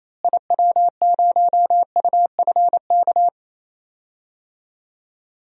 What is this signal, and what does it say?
Morse code, heard as a keyed tone, about 28 wpm, 700 Hz. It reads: IW0UFK
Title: DXCC-100 Call Signs - 6 Characters - Repeated 2x - 28wpm